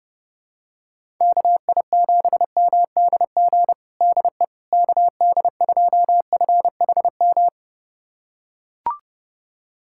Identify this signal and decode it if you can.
Morse code, heard as a keyed tone, about 30 wpm, 700 Hz. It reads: KI7MDG DE KD2FHM
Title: USA Callsign DE Callsign 30wpm